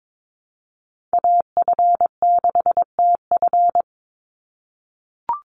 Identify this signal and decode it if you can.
Morse code, heard as a keyed tone, about 22 wpm, 700 Hz. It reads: AF6TF
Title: USA Call Signs 22wpm